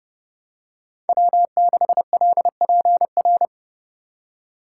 Morse code, heard as a keyed tone, about 29 wpm, 700 Hz. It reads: W6LPR